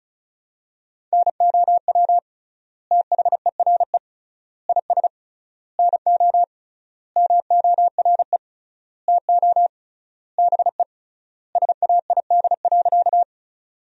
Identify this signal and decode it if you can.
Morse code, heard as a keyed tone, about 34 wpm, 700 Hz. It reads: NOW THERE IS NO MORE TO BE SAID.
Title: 3x Word-Spacing - Sentences from Top 100 Words - 35wpm